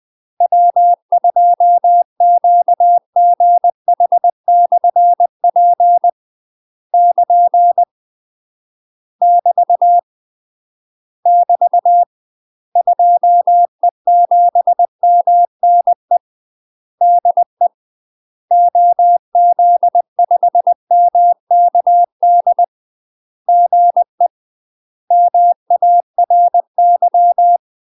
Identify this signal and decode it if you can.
Morse code, heard as a keyed tone, about 20 wpm, 700 Hz. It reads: W2QGH/P <KN> <BT> <BT> 2E7MNE DE OZ5MKD GE MARY
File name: UK MASTER QSOs TRAINING / 20WPM / UK MASTER QSOs 0053